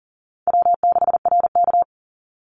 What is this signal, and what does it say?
Morse code, heard as a keyed tone, about 40 wpm, 700 Hz. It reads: W6RK
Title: Easy Call Signs Repeated 2x - 40wpm